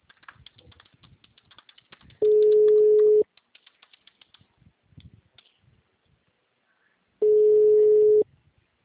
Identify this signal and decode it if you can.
Ringback tone (Europe, 425 Hz), with speech.